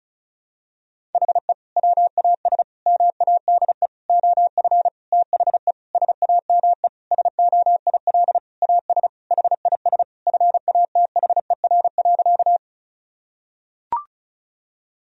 Morse code, heard as a keyed tone, about 36 wpm, 700 Hz. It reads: HE WAS MADE OF THE SAME SOIL AS HIS FATHER.
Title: Sentences from Top 1000 Words - Review Part 4 - 35wpm